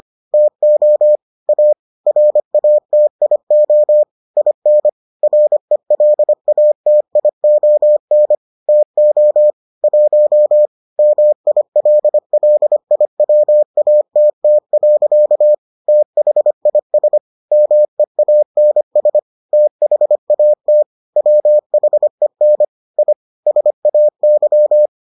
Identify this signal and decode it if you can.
Morse code, heard as a keyed tone, about 25 wpm, 600 Hz. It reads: TO A RATIO IN RELATION TO 1 MILLIWATT. THIS MEANS THAT WHEN I SAY